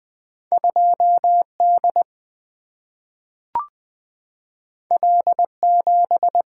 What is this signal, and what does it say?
Morse code, heard as a keyed tone, about 20 wpm, 700 Hz. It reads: 2D L7
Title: Single-Letter Single-Number 20wpm